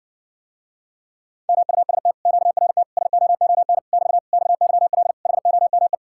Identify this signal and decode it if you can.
Morse code, heard as a keyed tone, about 60 wpm, 700 Hz. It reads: MUST QRT SOON <BT> XYL HOME